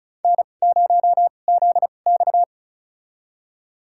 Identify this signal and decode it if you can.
Morse code, heard as a keyed tone, about 36 wpm, spaced as for 25 wpm, 700 Hz. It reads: N0ZX
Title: USA Call Signs - 4 Characters - 2x Character Spacing - 35wpm